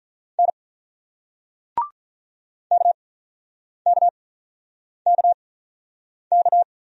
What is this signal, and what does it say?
Morse code, heard as a keyed tone, about 48 wpm, spaced as for 9 wpm, 700 Hz. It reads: N KKKK